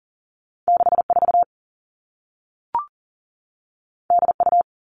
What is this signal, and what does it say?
Morse code, heard as a keyed tone, about 38 wpm, 700 Hz. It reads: 6 4 D U
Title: Two Challenging Letter-Number Combinations 40wpm